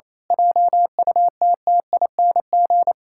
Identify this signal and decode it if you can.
Morse code, heard as a keyed tone, about 28 wpm, 700 Hz. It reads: JUTTING
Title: Pangram Challenge v2 - 28wpm